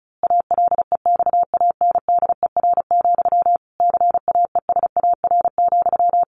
Morse code, heard as a keyed tone, about 34 wpm, 700 Hz. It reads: ALEXANDER, CAESAR,
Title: A Sound Of Thunder 35wpm